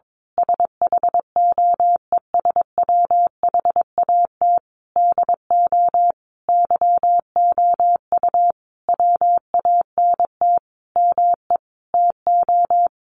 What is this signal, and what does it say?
Morse code, heard as a keyed tone, about 22 wpm, 700 Hz. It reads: SHOES WHAT DO YOU WANT ME TO
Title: A Sound Of Thunder 22wpm